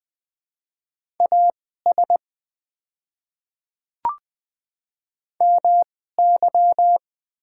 Morse code, heard as a keyed tone, about 20 wpm, spaced as for 14 wpm, 700 Hz. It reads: AS MY
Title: 2x Character Spacing - Top 100 Words - 20wpm